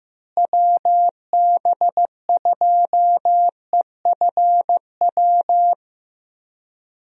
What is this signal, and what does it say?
Morse code, heard as a keyed tone, about 15 wpm, 700 Hz. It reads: WB2EFW
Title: USA Call Signs - 6 Characters - Repeated 3x - 15wpm